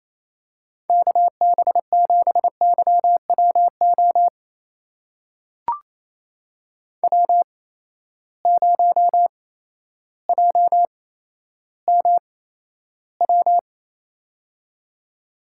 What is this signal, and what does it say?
Morse code, heard as a keyed tone, about 27 wpm, spaced as for 13 wpm, 700 Hz. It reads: KB7YWO W0JMW
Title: USA Call Signs - 8x Character Spacing - 28wpm